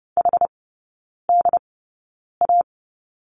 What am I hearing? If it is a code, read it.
Morse code, heard as a keyed tone, about 30 wpm, spaced as for 9 wpm, 700 Hz. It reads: HDA
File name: Lesson 15e - Number 5 - Three Characters 30wpm